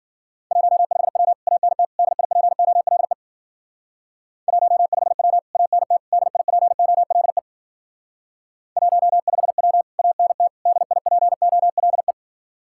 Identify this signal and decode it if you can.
Morse code, heard as a keyed tone, about 53 wpm, 700 Hz. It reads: 15W ANT DIPOLE 15W ANT DIPOLE 15W ANT DIPOLE